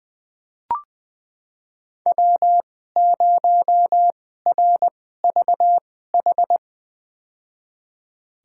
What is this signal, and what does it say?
Morse code, heard as a keyed tone, about 20 wpm, spaced as for 14 wpm, 700 Hz. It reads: W0RVH
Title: USA Call Signs - 2x Character Spacing - 20wpm